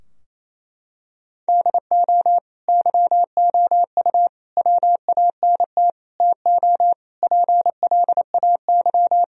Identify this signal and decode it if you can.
Morse code, heard as a keyed tone, about 28 wpm, 700 Hz. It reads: DO YOU WANT TO PLAY
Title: Sentences from Top 1000 Words - Review Part 3 - 28wpm